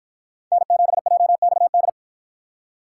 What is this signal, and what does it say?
Morse code, heard as a keyed tone, about 53 wpm, 700 Hz. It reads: N7JXD